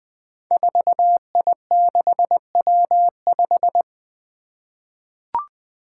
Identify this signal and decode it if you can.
Morse code, heard as a keyed tone, about 21 wpm, 700 Hz. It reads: 4I6W5
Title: Five Challenging Letter-Number Combinations 20wpm